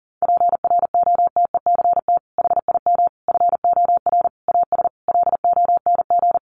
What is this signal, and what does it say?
Morse code, heard as a keyed tone, about 40 wpm, 700 Hz. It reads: PROTECT HIM FOR AS LONG
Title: Sentences from Top 900 Words 40wpm